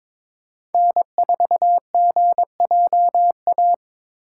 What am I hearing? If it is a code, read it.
Morse code, heard as a keyed tone, about 21 wpm, 700 Hz. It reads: N4GJA
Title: USA Call Signs - 5 Characters - 3x Character Spacing - 22wpm